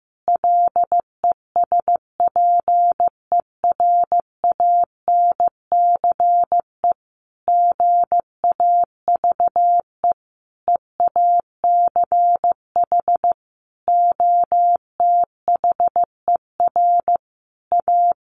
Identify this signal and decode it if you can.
Morse code, heard as a keyed tone, about 15 wpm, 700 Hz. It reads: LESPERANCE GAVE EACH OTHER A
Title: A Sound Of Thunder 15wpm